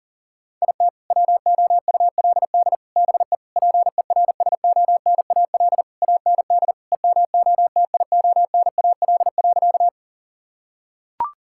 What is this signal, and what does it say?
Morse code, heard as a keyed tone, about 40 wpm, 700 Hz. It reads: IT WOULD BE PERSONAL AND EMOTIONAL.
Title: Sentences from Top 1000 Words - Review Part 5 - 40wpm